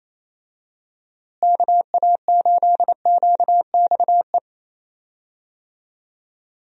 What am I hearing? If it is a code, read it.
Morse code, heard as a keyed tone, about 28 wpm, 700 Hz. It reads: KA8QXE
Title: USA Call Signs - 6 Characters - Repeated 3x - 28wpm